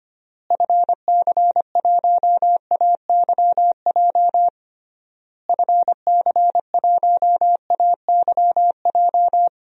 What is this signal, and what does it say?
Morse code, heard as a keyed tone, about 25 wpm, 700 Hz. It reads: FC1AYJ FC1AYJ
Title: DXCC-100 Call Signs - 6 Characters - Repeated 3x - 25wpm